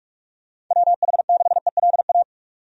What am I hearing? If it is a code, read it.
Morse code, heard as a keyed tone, about 42 wpm, 700 Hz. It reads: WH6ELA
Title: USA Call Signs - 6 Characters - 45wpm